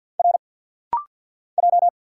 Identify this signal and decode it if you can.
Morse code, heard as a keyed tone, about 50 wpm, 700 Hz. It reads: R J
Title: Single Letters - Rapid-Fire 50wpm